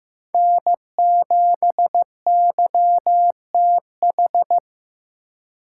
Morse code, heard as a keyed tone, about 18 wpm, 700 Hz. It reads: N7YTH
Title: Speed Racing - USA 1x3 Call Signs 15wpm